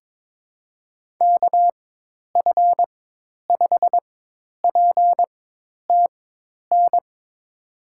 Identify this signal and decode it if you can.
Morse code, heard as a keyed tone, about 22 wpm, spaced as for 10 wpm, 700 Hz. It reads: KF5PTN
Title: USA Call Signs - 6 Characters - 4x Character Spacing - 22wpm